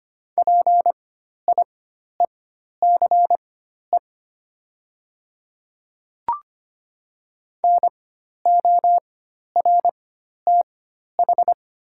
Morse code, heard as a keyed tone, about 25 wpm, spaced as for 12 wpm, 690 Hz. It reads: PIECE NORTH
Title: 4x Character Spacing - Top 500 Words - 25wpm